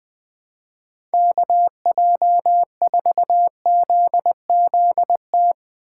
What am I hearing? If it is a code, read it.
Morse code, heard as a keyed tone, about 20 wpm, 700 Hz. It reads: KJ4ZZT
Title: USA Call Signs - 2x Character Spacing - 20wpm